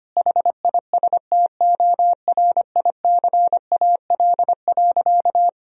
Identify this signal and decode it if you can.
Morse code, heard as a keyed tone, about 25 wpm, 700 Hz. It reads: HISTORICAL.
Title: Sentences from Top 1000 Words - Review Part 4 - 25wpm